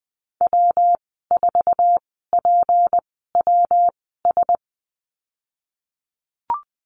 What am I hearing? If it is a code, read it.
Morse code, heard as a keyed tone, about 20 wpm, spaced as for 15 wpm, 700 Hz. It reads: W4PWS
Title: 161 Lesson 26g - Letter g - Call Signs 20wpm